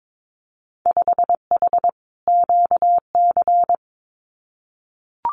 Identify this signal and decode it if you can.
Morse code, heard as a keyed tone, about 22 wpm, 700 Hz. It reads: 5H QC
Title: Winter Field Day Exchanges Encore 22wpm